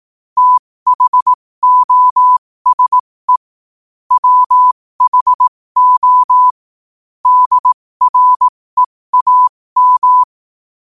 Morse code, heard as a keyed tone, about 18 wpm, spaced as for 15 wpm, 1000 Hz. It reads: THOSE WHO DREAM
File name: 15wpm